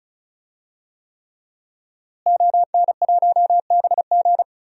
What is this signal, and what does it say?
Morse code, heard as a keyed tone, about 34 wpm, 700 Hz. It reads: ON1BG
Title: DXCC-100 Call Signs - 5 Characters - Repeated 3x - 35wpm